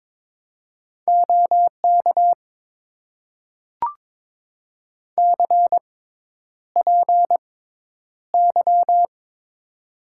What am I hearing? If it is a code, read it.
Morse code, heard as a keyed tone, about 22 wpm, spaced as for 10 wpm, 700 Hz. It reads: OK CPY
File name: Top 90 QSO Elements - 6x Character Spacing - 22wpm